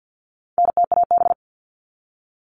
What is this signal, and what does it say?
Morse code, heard as a keyed tone, about 50 wpm, 700 Hz. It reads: NTAB
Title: All Cut Numbers - 4-Digit 50wpm